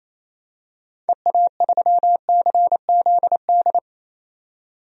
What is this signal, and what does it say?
Morse code, heard as a keyed tone, about 28 wpm, 700 Hz. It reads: EA3CZD
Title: DXCC-100 Call Signs - 6 Characters - 28wpm